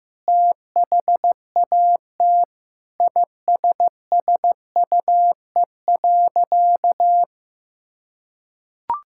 Morse code, heard as a keyed tone, about 15 wpm, 700 Hz. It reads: THAT ISSUE.
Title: Sentences from Top 1000 Words - Review Part 6 - 15wpm